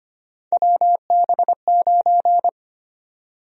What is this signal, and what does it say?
Morse code, heard as a keyed tone, about 25 wpm, 700 Hz. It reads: WB9
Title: USA Call Sign Prefixes - Repeated 2x - 25wpm